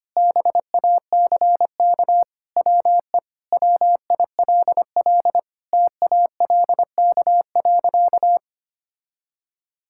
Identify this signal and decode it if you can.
Morse code, heard as a keyed tone, about 25 wpm, 700 Hz. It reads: BACK WE WILL TALK.
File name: practice / 25wpm / Sentences with 4-Word Phrases - 25wpm